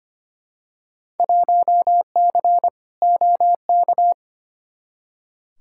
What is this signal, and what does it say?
Morse code, heard as a keyed tone, about 25 wpm, 700 Hz. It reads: 1C OK
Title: Field Day Exchanges Encore - 25wpm